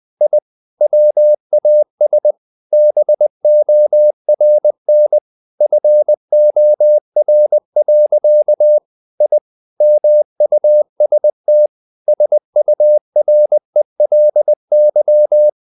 Morse code, heard as a keyed tone, about 20 wpm, 600 Hz. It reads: I WAS BORN FOR. I MUST SURELY
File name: the old man and the sea / the old man and the sea part 1 20wpm